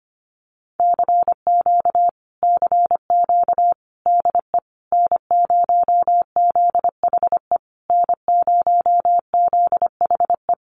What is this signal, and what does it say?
Morse code, heard as a keyed tone, about 25 wpm, 700 Hz. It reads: CQ CQ DE N0ZHE N0ZHE
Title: Common QSO Phrases Encore 25wpm